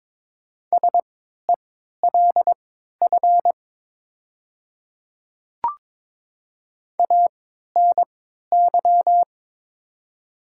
Morse code, heard as a keyed tone, about 22 wpm, spaced as for 13 wpm, 700 Hz. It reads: SELF ANY
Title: 3x Character Spacing - Top 200 Words - 22wpm